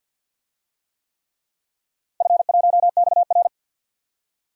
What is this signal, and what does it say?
Morse code, heard as a keyed tone, about 53 wpm, 700 Hz. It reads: F1XR
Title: DXCC-100 Call Signs - 4 Characters - 50wpm